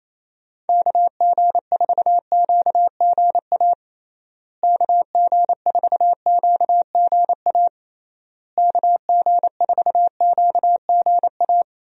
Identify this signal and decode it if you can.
Morse code, heard as a keyed tone, about 28 wpm, 700 Hz. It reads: KG4QGA KG4QGA KG4QGA
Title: USA Call Signs - 6 Characters - Repeated 3x - 28wpm